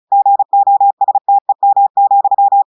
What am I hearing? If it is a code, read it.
Morse code, heard as a keyed tone, about 36 wpm, 800 Hz. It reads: GOSTEM,
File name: audios / portugais / portugais texte en clair 2 vitesse 35 mots minutes